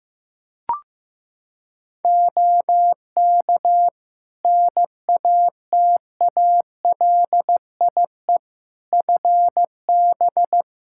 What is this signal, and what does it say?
Morse code, heard as a keyed tone, about 15 wpm, 700 Hz. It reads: OK NATALIE FB
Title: Practice QSO - Exchange 2 - Info - 15wpm